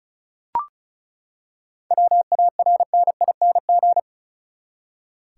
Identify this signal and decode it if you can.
Morse code, heard as a keyed tone, about 36 wpm, 700 Hz. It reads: WARNING
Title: Seven Letter Words - 35wpm